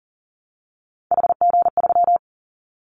Morse code, heard as a keyed tone, about 40 wpm, 700 Hz. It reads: HG3
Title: Callsign Prefixes for Top 25 DXCC - 40wpm